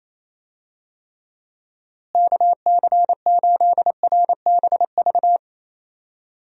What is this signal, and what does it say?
Morse code, heard as a keyed tone, about 27 wpm, 700 Hz. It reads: KC8RBV
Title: USA Call Signs - 6 Characters - Repeated 2x - 28wpm